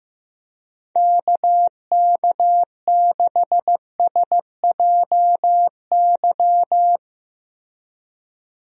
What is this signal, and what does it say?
Morse code, heard as a keyed tone, about 15 wpm, 700 Hz. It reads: KK6SJY
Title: USA Call Signs - 6 Characters - 15wpm